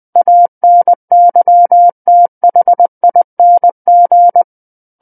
Morse code, heard as a keyed tone, about 20 wpm, 700 Hz. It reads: ANYTHING
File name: 20 WPM